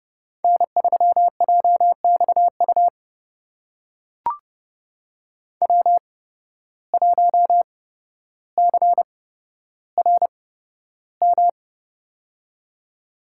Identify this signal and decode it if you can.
Morse code, heard as a keyed tone, about 30 wpm, spaced as for 13 wpm, 700 Hz. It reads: N3JXU W1CRM
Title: USA Call Signs - 5 Characters - 8x Character Spacing - 30wpm